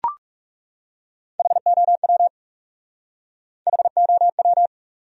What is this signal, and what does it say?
Morse code, heard as a keyed tone, about 39 wpm, 700 Hz. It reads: HOW HOW